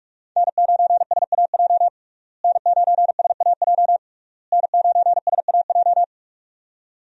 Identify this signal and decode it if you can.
Morse code, heard as a keyed tone, about 41 wpm, 700 Hz. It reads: N9SAJ N9SAJ N9SAJ
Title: USA Call Signs - 5 Characters - Repeated 3x - 45wpm